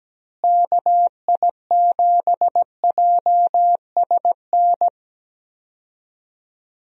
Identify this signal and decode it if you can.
Morse code, heard as a keyed tone, about 17 wpm, 700 Hz. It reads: KI7JSN
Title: USA Call Signs - 6 Characters - Repeated 3x - 17wpm